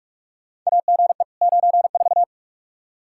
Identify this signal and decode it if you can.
Morse code, heard as a keyed tone, about 46 wpm, 700 Hz. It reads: AGE 94
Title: Practice QSO - Exchange 3 - More Info - 45wpm